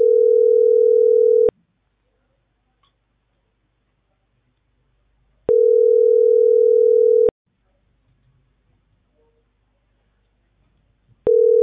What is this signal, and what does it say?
Ringback tone (North America), 440 + 480 Hz.